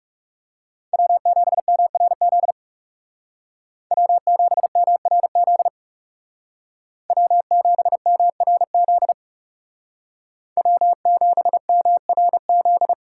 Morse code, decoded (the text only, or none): W7MRZ W7MRZ W7MRZ W7MRZ